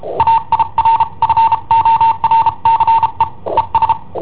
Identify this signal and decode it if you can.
Morse code, heard as a keyed tone, about 33 wpm, spaced as for 29 wpm, 910 Hz. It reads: AIRFORCE ES
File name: i7ohp / cassetta i7ohp B